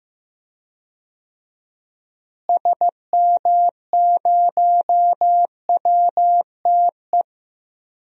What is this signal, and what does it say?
Morse code, heard as a keyed tone, about 15 wpm, 700 Hz. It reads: SM0WTE
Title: DXCC-100 Call Signs - 6 Characters - Repeated 2x - 15wpm